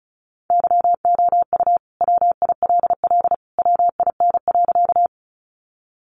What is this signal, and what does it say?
Morse code, heard as a keyed tone, about 34 wpm, 700 Hz. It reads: YOU WILL WIN.
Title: Sentences from Top 1500 Words 35wpm